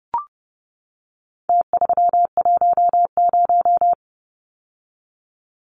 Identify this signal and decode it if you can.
Morse code, heard as a keyed tone, about 30 wpm, 700 Hz. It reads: T310
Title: Numbers - Contest Style - 4-Digit 30wpm